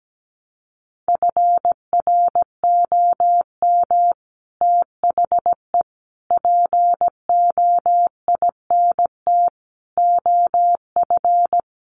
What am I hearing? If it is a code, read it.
Morse code, heard as a keyed tone, about 17 wpm, 700 Hz. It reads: FROM THE POINT OF
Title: Sets of 5 Words 17wpm